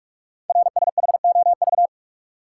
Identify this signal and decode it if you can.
Morse code, heard as a keyed tone, about 44 wpm, 700 Hz. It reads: RSHOV
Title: Code Groups - 5 Letter:Numbers - 45wpm